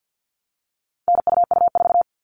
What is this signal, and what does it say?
Morse code, heard as a keyed tone, about 50 wpm, 700 Hz. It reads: NUU4
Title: All Cut Numbers - 4-Digit 50wpm